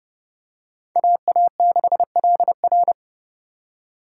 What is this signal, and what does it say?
Morse code, heard as a keyed tone, about 29 wpm, 700 Hz. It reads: AA6LR